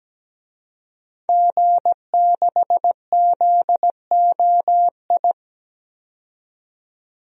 Morse code, heard as a keyed tone, about 17 wpm, 700 Hz. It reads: G6ZOI
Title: DXCC-100 Call Signs - 5 Characters - 17wpm